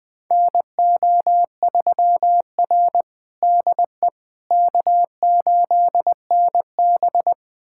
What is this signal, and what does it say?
Morse code, heard as a keyed tone, about 20 wpm, 700 Hz. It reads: NO3R DE K8NB
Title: USA Callsign DE Callsign - 4 Characters - 20wpm